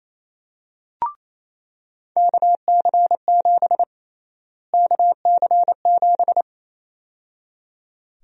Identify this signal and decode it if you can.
Morse code, heard as a keyed tone, about 28 wpm, 700 Hz. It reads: KC7 KC7
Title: USA Call Sign Prefixes - Repeated 2x - 28wpm